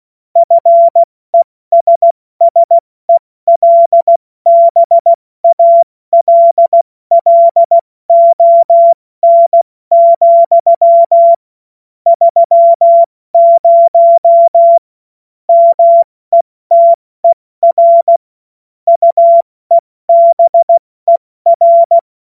Morse code, heard as a keyed tone, about 16 wpm, spaced as for 14 wpm, 680 Hz. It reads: FESSELBALLON, 30 METER UEBER
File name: Morsekurs / Witze / Witz 16. August 70 F80